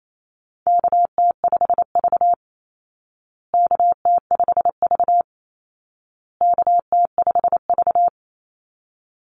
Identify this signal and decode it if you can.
Morse code, heard as a keyed tone, about 28 wpm, 700 Hz. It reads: KT5V KT5V KT5V
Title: Call Signs Repeated 2x - 28wpm